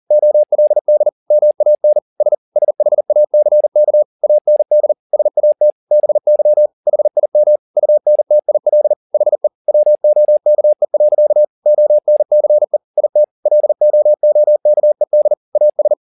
Morse code, heard as a keyed tone, about 40 wpm, 600 Hz. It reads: OLD MAN S SHACK AND SAT BY HIM UNTIL HE WOKE. ONCE IT LOOKED AS
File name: the old man and the sea 2 40wpm